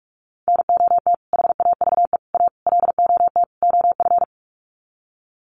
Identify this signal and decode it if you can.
Morse code, heard as a keyed tone, about 44 wpm, 700 Hz. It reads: NOT HAVE A LOT OF
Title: Sets of 5 Words 45wpm